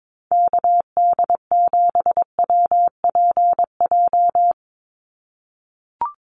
Morse code, heard as a keyed tone, about 22 wpm, 700 Hz. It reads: KD7WPJ
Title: NA SOTA Activator Callsigns 22wpm